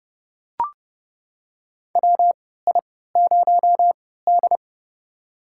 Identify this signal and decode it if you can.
Morse code, heard as a keyed tone, about 31 wpm, spaced as for 17 wpm, 700 Hz. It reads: WI0D
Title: USA Call Signs - 4 Characters - 3x Character Spacing - 30wpm